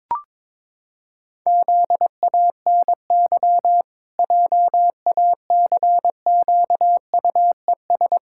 Morse code, heard as a keyed tone, about 21 wpm, 700 Hz. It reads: ZANY JACQUES
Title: Pangram Challenge v2 - 22wpm